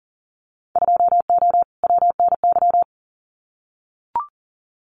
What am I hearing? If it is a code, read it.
Morse code, heard as a keyed tone, about 40 wpm, 700 Hz. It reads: 2O WNY
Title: Winter Field Day Exchanges Encore 40wpm